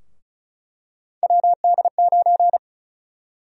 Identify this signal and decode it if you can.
Morse code, heard as a keyed tone, about 36 wpm, 700 Hz. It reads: WD9
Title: USA Call Sign Prefixes - 35wpm